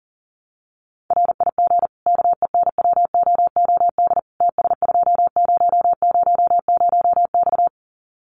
Morse code, heard as a keyed tone, about 40 wpm, 700 Hz. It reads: RIG KENWOOD TS2000X
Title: Common QSO Phrases 40wpm